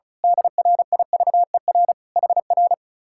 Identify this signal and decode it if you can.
Morse code, heard as a keyed tone, about 32 wpm, 700 Hz. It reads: DRIVER HR
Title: Speed-Racing Common QSO Phrases - 35wpm